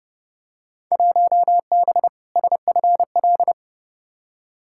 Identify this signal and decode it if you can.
Morse code, heard as a keyed tone, about 30 wpm, 700 Hz. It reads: 1B SFL
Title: Field Day Exchanges Encore - 30wpm